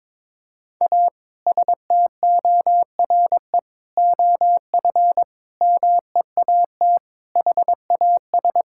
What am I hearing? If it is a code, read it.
Morse code, heard as a keyed tone, about 22 wpm, 700 Hz. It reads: A STORE OF MEAT HAS
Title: Sentences from Top 1000 Words - Review Part 5 - 22wpm